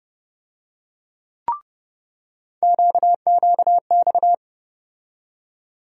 Morse code, heard as a keyed tone, about 30 wpm, 700 Hz. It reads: QQX
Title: USA Call Sign Postfixes - 3 Letters 30wpm